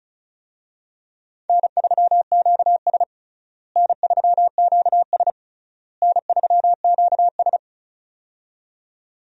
Morse code, heard as a keyed tone, about 36 wpm, 700 Hz. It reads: N3QS N3QS N3QS
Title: USA Call Signs - 4 Characters - Repeated 3x - 35wpm